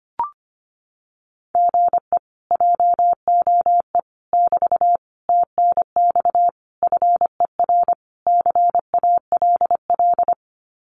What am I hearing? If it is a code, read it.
Morse code, heard as a keyed tone, about 25 wpm, 700 Hz. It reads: GE JOE <BT> TNX FER CALL
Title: Common QSO Phrases Encore 25wpm